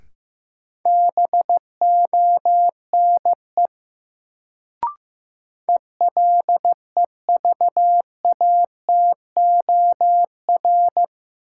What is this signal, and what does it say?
Morse code, heard as a keyed tone, about 15 wpm, 700 Hz. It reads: BONE ELEVATOR
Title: Sentences from Top 1600 Words - New Word Drill - 15wpm